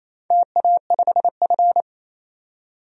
Morse code, heard as a keyed tone, about 27 wpm, 690 Hz. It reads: TA5F
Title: DXCC-100 Call Signs - 4 Characters - 28wpm